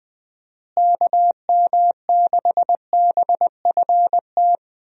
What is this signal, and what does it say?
Morse code, heard as a keyed tone, about 20 wpm, 700 Hz. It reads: KM6BFT
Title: USA Call Signs - 6 Characters - Repeated 3x - 20wpm